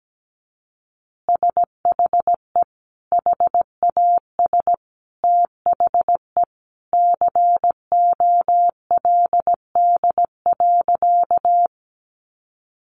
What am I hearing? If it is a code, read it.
Morse code, heard as a keyed tone, about 17 wpm, 700 Hz. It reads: SHE HAS THE COLD.
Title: Easy Phrases 17wpm